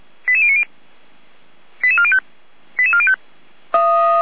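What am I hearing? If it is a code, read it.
Five-tone selective call (ZVEI), heard as quick runs of five tones, tones of about 70 ms. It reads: 88090 88405 88405
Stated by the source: ZVEI five-tone call received by SDR, station feuerwehr-orlamuende.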